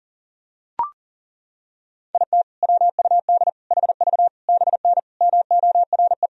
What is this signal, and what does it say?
Morse code, heard as a keyed tone, about 40 wpm, 700 Hz. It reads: IT WUD HV BN MORE
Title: Sentences from Top 400 Words and Abbreviations 40wpm